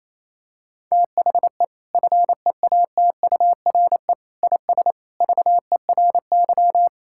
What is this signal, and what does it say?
Morse code, heard as a keyed tone, about 28 wpm, 700 Hz. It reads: THE FEATURE IS VERY
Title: Sentences from Top 1000 Words - Review Part 5 - 28wpm